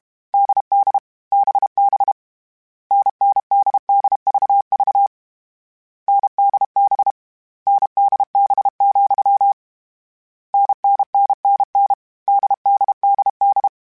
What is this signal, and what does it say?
Morse code, heard as a keyed tone, about 32 wpm, 800 Hz. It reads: DD BB NNDDVV NDB NDB, NNNNN DDDD
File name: Difficult Combos / Difficult Combos-mp3s / NDB NDB6 @30